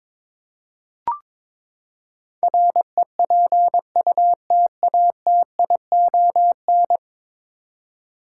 Morse code, heard as a keyed tone, about 22 wpm, 700 Hz. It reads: REPUTATION
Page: morse-code-ninja-practice-22wpm